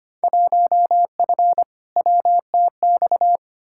Morse code, heard as a keyed tone, about 25 wpm, 700 Hz. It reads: 1F WTX
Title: Field Day Exchanges Encore - 25wpm